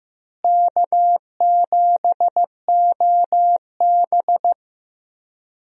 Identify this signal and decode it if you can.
Morse code, heard as a keyed tone, about 15 wpm, 700 Hz. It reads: K7OB
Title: USA Call Signs - 4 Characters - Repeated 2x - 15wpm